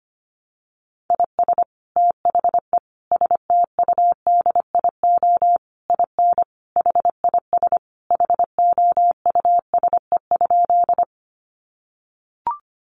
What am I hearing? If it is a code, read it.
Morse code, heard as a keyed tone, about 25 wpm, 700 Hz. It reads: IS THE STUDIO IN HIS HOUSE?
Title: Sentences from Top 1200 Words 25wpm